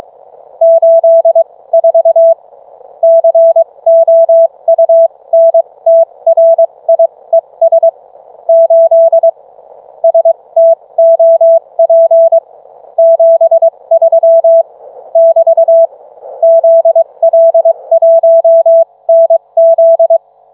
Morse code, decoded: 84 COUNTRIES 8 STOP 73 <BT> ZL1NZ